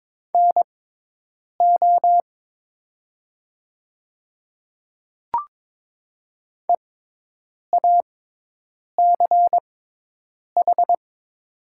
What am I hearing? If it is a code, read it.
Morse code, heard as a keyed tone, about 22 wpm, spaced as for 8 wpm, 700 Hz. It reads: NO EACH
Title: 6x Character Spacing - Top 100 Words - 22wpm